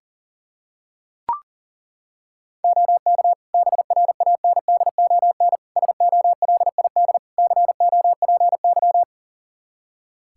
Morse code, heard as a keyed tone, about 40 wpm, 700 Hz. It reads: OK BRANDON SOLID COPY
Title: K1USN SST QSO Practice 40wpm